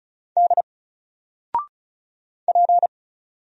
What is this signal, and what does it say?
Morse code, heard as a keyed tone, about 35 wpm, 700 Hz. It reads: D P
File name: Single-Letter-Number-35wpm